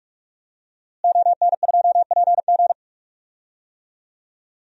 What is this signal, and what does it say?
Morse code, heard as a keyed tone, about 46 wpm, 700 Hz. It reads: ON2PG
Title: DXCC-100 Call Signs - 5 Characters - Repeated 3x - 45wpm